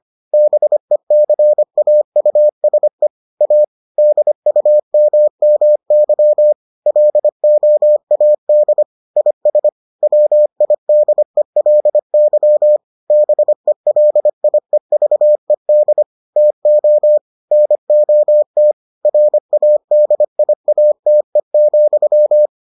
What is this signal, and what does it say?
Morse code, heard as a keyed tone, about 25 wpm, 600 Hz. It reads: BECAUSE A DUMMY LOAD IS WIDELY BELIEVED TO NOT RADIATE,